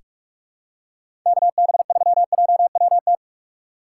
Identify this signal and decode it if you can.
Morse code, heard as a keyed tone, about 45 wpm, 690 Hz. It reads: KB3JWT
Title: USA Call Signs - 6 Characters - 3x Character Spacing - 45wpm